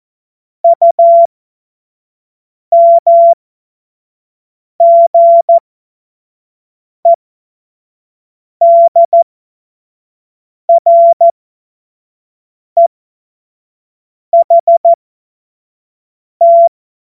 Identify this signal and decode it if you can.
Morse code, heard as a keyed tone, about 14 wpm, spaced as for 5 wpm, 680 Hz. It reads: UMGEDREHT
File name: Morsekurs / Witze / Witz 17. Mai 2022 25 F70